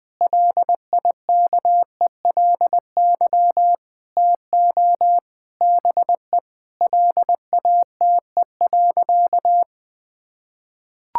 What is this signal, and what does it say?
Morse code, heard as a keyed tone, about 20 wpm, 700 Hz. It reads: LIKELY TO BE LATE.
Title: Sentences with 4-Word Phrases - 20wpm